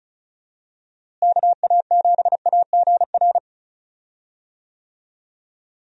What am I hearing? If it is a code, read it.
Morse code, heard as a keyed tone, about 34 wpm, 700 Hz. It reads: KA7AGR
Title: USA Call Signs - 6 Characters - Repeated 2x - 35wpm